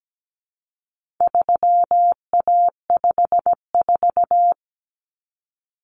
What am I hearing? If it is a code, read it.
Morse code, heard as a keyed tone, about 17 wpm, 700 Hz. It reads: 3A54
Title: Common Cut Numbers - 4-Digit 17wpm